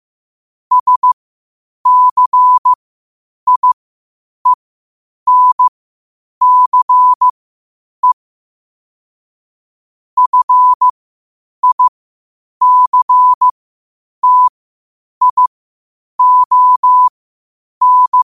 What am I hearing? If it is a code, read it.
Morse code, heard as a keyed tone, about 15 wpm, spaced as for 8 wpm, 1000 Hz. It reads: SCIENCE FICTION
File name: qotdfiles / 8wpm